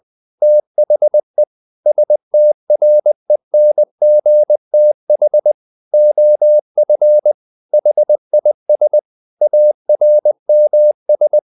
Morse code, heard as a keyed tone, about 20 wpm, 600 Hz. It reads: THE STRENGTH OF HIS ARMS
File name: the old man and the sea / the old man and the sea part 1 20wpm